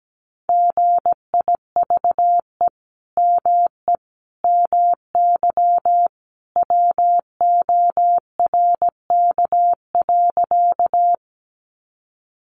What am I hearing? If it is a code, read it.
Morse code, heard as a keyed tone, about 17 wpm, 700 Hz. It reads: GIVE ME MY WORK.
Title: Sentences from Top 200 Words 17wpm